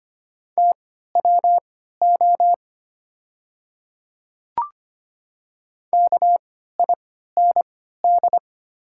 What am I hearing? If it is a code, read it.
Morse code, heard as a keyed tone, about 25 wpm, spaced as for 14 wpm, 700 Hz. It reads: TWO KIND
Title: 3x Character Spacing - Top 300 Words - 25wpm